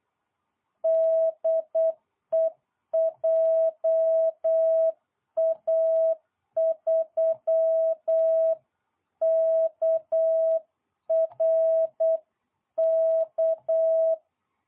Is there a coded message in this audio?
Yes: Morse code from 0.8 to 14.2 s, heard as a keyed tone, about 8 wpm, 660 Hz. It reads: DE JA3KRK